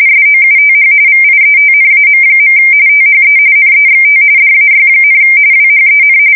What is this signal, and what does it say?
RTTY, about 45.45 baud. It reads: SOUTH VIETNAM. THE NEXT DAY, WHEN
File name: fsk170 / USS-New-Jersey